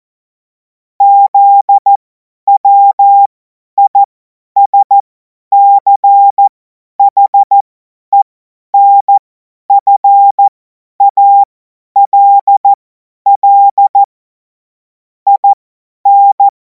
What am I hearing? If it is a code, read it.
Morse code, heard as a keyed tone, about 14 wpm, spaced as for 10 wpm, 800 Hz. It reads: ZWISCHENFALL IN
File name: Morsekurs / Witze / Witz 05. Mai 2026 50 F70